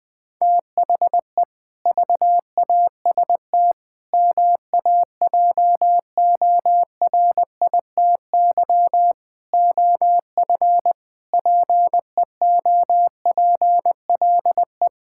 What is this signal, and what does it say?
Morse code, heard as a keyed tone, about 20 wpm, 700 Hz. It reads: THE VAST MAJORITY OF PEOPLE
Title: Sentences with 4-Word Phrases - 20wpm